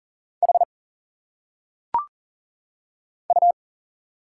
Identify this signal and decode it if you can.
Morse code, heard as a keyed tone, about 40 wpm, 700 Hz. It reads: H U
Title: Single Letters - Phonetic Pronunciation 40wpm